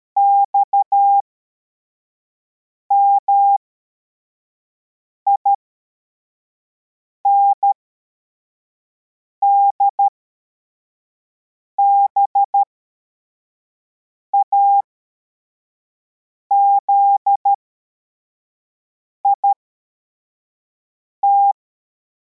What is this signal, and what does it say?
Morse code, heard as a keyed tone, about 13 wpm, spaced as for 4 wpm, 800 Hz. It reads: XMIND BAZIT